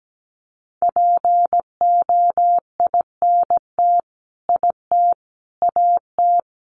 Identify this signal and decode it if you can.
Morse code, heard as a keyed tone, about 17 wpm, 700 Hz. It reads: POINT IT AT
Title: Sentences from Top 300 Words 17wpm